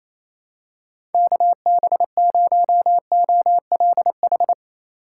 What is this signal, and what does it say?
Morse code, heard as a keyed tone, about 28 wpm, 700 Hz. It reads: KB0OLH
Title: USA Call Signs - 6 Characters - 6x Character Spacing - 28wpm